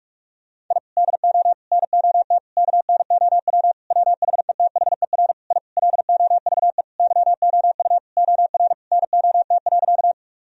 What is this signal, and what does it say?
Morse code, heard as a keyed tone, about 46 wpm, 700 Hz. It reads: I DO NOT KNOW WHETHER I LOVE YOU OR NOT.
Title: Sentences from Top 1000 Words - Review Part 3 - 45wpm